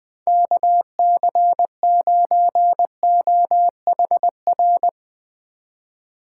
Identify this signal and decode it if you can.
Morse code, heard as a keyed tone, about 20 wpm, 700 Hz. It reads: KC9OHR
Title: USA Call Signs - 3x Character Spacing - 20wpm